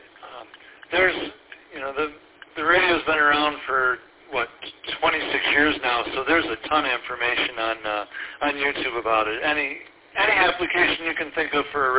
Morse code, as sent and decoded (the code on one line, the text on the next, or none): none